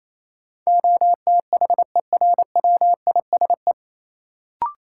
Morse code, heard as a keyed tone, about 28 wpm, 700 Hz. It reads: OTHERWISE
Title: Sentences from Top 1200 Words - New Word Drill - 28wpm